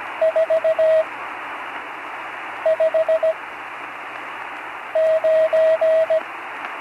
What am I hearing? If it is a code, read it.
Morse code, heard as a keyed tone, about 17 wpm, spaced as for 5 wpm, 640 Hz, with radio noise. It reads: 459
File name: CW Learning Programs / Learning Morse Code (from Tune in the World) / 43 on the air